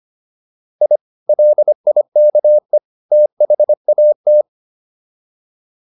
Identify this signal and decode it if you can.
Morse code, heard as a keyed tone, about 25 wpm, 600 Hz. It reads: I LIKE THAT